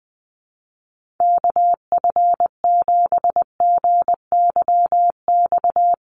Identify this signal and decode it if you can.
Morse code, heard as a keyed tone, about 20 wpm, 700 Hz. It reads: KF7GYX
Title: USA Call Signs Repeated 3x 20wpm